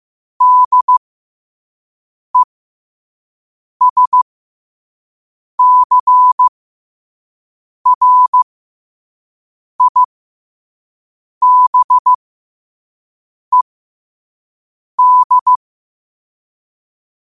Morse code, heard as a keyed tone, about 15 wpm, spaced as for 5 wpm, 1000 Hz. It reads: DESCRIBED
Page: New quotes every day in morse code at 5 Words per minute.